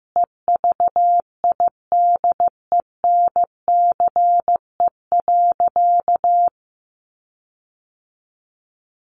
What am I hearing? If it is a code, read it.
Morse code, heard as a keyed tone, about 15 wpm, 700 Hz. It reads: EVIDENCE.
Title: Sentences from Top 1300 Words 15wpm